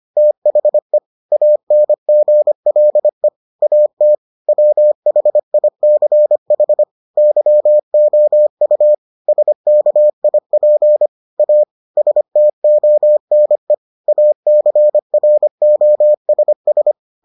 Morse code, heard as a keyed tone, about 25 wpm, 600 Hz. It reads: THE ANGLE AT WHICH YOU SKIP A STONE ACROSS